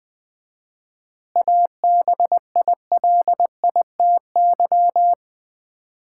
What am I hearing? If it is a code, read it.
Morse code, heard as a keyed tone, about 20 wpm, 700 Hz. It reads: ABILITY
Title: 4x Word-Spacing - Sets of 3 Words - 20wpm